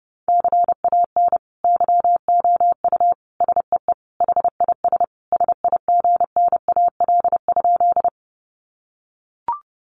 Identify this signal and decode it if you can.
Morse code, heard as a keyed tone, about 30 wpm, 700 Hz. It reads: CAN YOU SEE HIS SIGNAL?
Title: Sentences from Top 1400 Words 30wpm